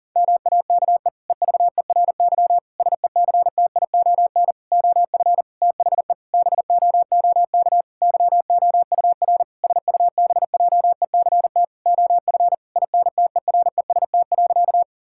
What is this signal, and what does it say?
Morse code, heard as a keyed tone, about 40 wpm, 700 Hz. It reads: MAKE EVERY SECTION OF THE BOOK YOUR SUBJECT OF INTEREST.
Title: Sentences from Top 1000 Words - Review Part 5 - 40wpm